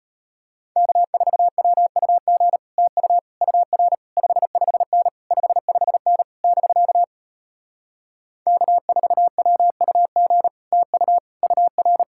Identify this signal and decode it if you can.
Morse code, heard as a keyed tone, about 33 wpm, 700 Hz. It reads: K4WUG TU UR 55N 55N <BK> K4WUG TU UR